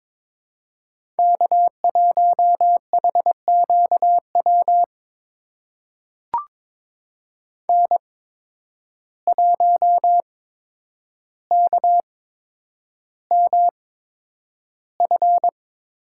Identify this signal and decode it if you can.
Morse code, heard as a keyed tone, about 22 wpm, spaced as for 9 wpm, 700 Hz. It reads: K1HQW N1KMF
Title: USA Call Signs - 5 Characters - 8x Character Spacing - 22wpm